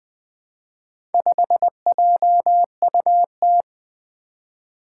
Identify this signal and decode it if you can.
Morse code, heard as a keyed tone, about 20 wpm, 700 Hz. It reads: 5JUT
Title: Four Challenging Letter-Number Combinations 20wpm